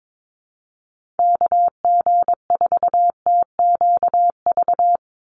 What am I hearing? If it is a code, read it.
Morse code, heard as a keyed tone, about 22 wpm, 700 Hz. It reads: KG4TQV
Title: USA Call Signs 22wpm